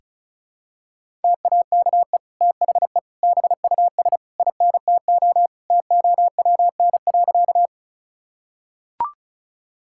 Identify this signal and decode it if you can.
Morse code, heard as a keyed tone, about 36 wpm, 700 Hz. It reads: TAKE THE BUS INTO TOWN.
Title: Sentences from Top 1000 Words - Review Part 6 - 35wpm